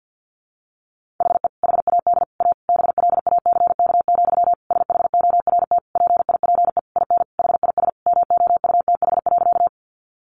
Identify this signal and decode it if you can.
Morse code, heard as a keyed tone, about 50 wpm, 700 Hz. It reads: HE HAD A BLACK, SHORT PIPE IN HIS MOUTH.
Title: Sentences from Top 1500 Words 50wpm